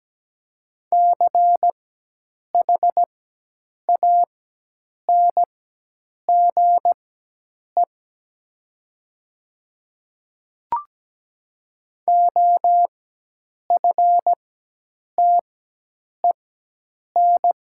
Morse code, heard as a keyed tone, about 17 wpm, spaced as for 8 wpm, 700 Hz. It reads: CHANGE OFTEN
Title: 4x Character Spacing - Top 300 Words - 17wpm